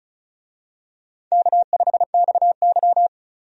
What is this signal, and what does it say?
Morse code, heard as a keyed tone, about 36 wpm, 700 Hz. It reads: K5XY
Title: USA Call Signs - 4 Characters - 35wpm